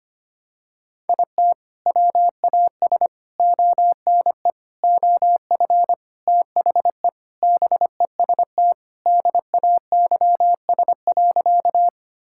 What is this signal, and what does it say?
Morse code, heard as a keyed tone, about 25 wpm, 700 Hz. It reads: IT WAS ONE OF THE BEST DAYS.
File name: practice / 25wpm / Sentences with 3-Word Phrases - 25wpm